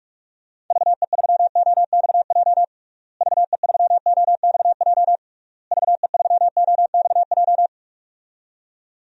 Morse code, heard as a keyed tone, about 42 wpm, 700 Hz. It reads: VE3OXJ VE3OXJ VE3OXJ
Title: DXCC-100 Call Signs - 6 Characters - Repeated 3x - 45wpm